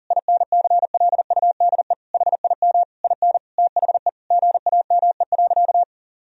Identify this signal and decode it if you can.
Morse code, heard as a keyed tone, about 40 wpm, 700 Hz. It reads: INCLUDE HIM IN THE GAME.
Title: Sentences from Top 1000 Words - Review Part 3 - 40wpm